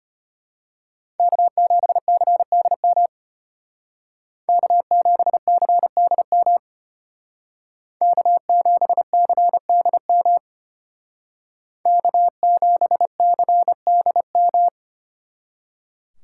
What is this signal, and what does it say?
Morse code, heard as a keyed tone, about 30 wpm, 700 Hz. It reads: K7CDM K7CDM K7CDM K7CDM
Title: Speed Racing - USA 1x3 Call Signs 25wpm